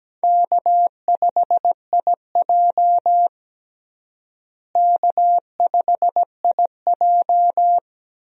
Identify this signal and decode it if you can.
Morse code, heard as a keyed tone, about 17 wpm, 700 Hz. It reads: K5IJ K5IJ
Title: USA Call Signs - 4 Characters - Repeated 2x - 17wpm